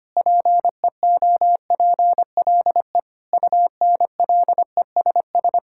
Morse code, heard as a keyed tone, about 25 wpm, 700 Hz. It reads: PEOPLE UNLESS
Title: Sentences from Top 1000 Words - Review Part 6 - 25wpm